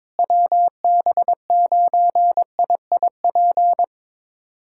Morse code, heard as a keyed tone, about 22 wpm, 700 Hz. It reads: WB9IIP
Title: USA Callsign DE Callsign - 6 Characters - 22wpm